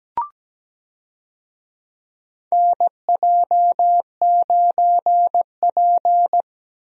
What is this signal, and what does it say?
Morse code, heard as a keyed tone, about 17 wpm, 700 Hz. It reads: NJ9P
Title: USA Call Signs - 4 Characters - 17wpm